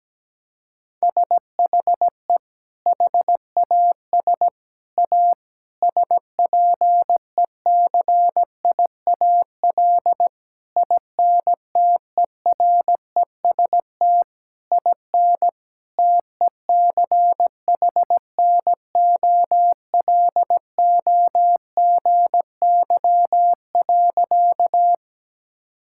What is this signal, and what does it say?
Morse code, heard as a keyed tone, about 17 wpm, 700 Hz. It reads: SHE HAS A SPECIAL INTEREST IN TECHNOLOGY.
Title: Sentences from Top 1000 Words - Review Part 5 - 17wpm